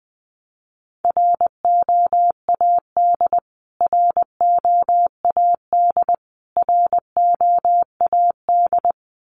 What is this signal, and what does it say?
Morse code, heard as a keyed tone, about 20 wpm, 700 Hz. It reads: ROAD ROAD ROAD
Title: Top 500 Words Repeated 20wpm